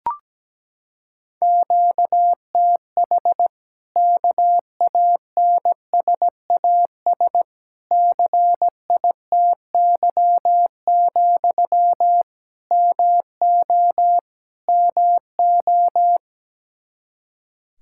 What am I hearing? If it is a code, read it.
Morse code, heard as a keyed tone, about 17 wpm, 700 Hz. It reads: QTH KANSAS CITY, MO MO
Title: Practice QSO - Exchange 1 - Beginning - 17wpm